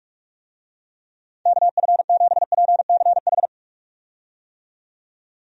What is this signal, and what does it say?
Morse code, heard as a keyed tone, about 45 wpm, 700 Hz. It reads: KF7PCH
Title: USA Call Signs - 6 Characters - 45wpm